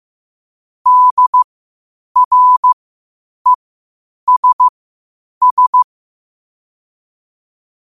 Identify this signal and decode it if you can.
Morse code, heard as a keyed tone, about 15 wpm, spaced as for 8 wpm, 1000 Hz. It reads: DRESS
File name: qotdfiles / 8wpm